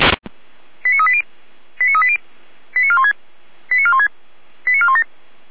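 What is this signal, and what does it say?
Five-tone selective call (ZVEI), heard as quick runs of five tones, tones of about 70 ms. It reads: 89380 89380 89416 89416 89417
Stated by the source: ZVEI five-tone call received by SDR, from the Feuerwehr Orlamünde fire brigade audio server.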